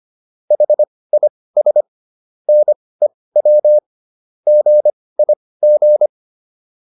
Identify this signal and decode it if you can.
Morse code, heard as a keyed tone, about 25 wpm, spaced as for 18 wpm, 600 Hz. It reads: HIS NEW GIG